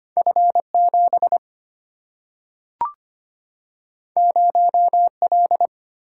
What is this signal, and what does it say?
Morse code, heard as a keyed tone, about 25 wpm, 700 Hz. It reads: F 7 0 L
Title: Single-Letter Single-Number 25wpm